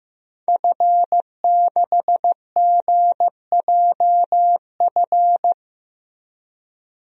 Morse code, heard as a keyed tone, about 15 wpm, 700 Hz. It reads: F6GJF
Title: DXCC-100 Call Signs - 5 Characters - Repeated 3x - 15wpm